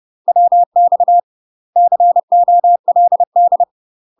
Morse code, heard as a keyed tone, about 30 wpm, 700 Hz. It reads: WX COLD
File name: UK MASTER QSOs TRAINING / 30WPM / UK MASTER QSOs 0027